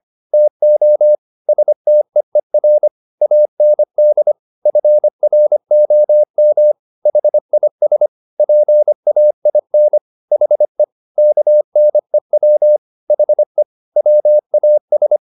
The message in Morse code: TO STEER AND FROM HIS PAIN HE KNEW HE WAS

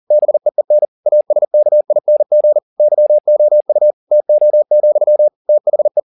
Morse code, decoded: BEEN ASKING YOU TO, THE

